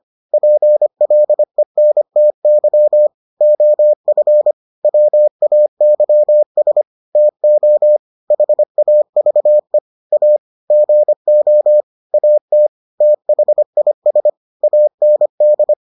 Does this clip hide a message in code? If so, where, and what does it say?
Yes: Morse code from 0.3 to 15.7 s, heard as a keyed tone, about 25 wpm, 600 Hz. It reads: PLENTY OF WAYS TO HAVE A GO AT THIS AND